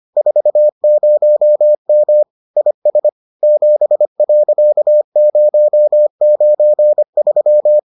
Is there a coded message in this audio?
Yes: Morse code from 0.2 to 7.8 s, heard as a keyed tone, about 25 wpm, 600 Hz. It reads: 40M IS 7.093